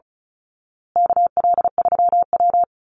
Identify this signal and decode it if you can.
Morse code, heard as a keyed tone, about 34 wpm, 700 Hz. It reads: KL3W
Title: Call Signs Repeated 2x - 35wpm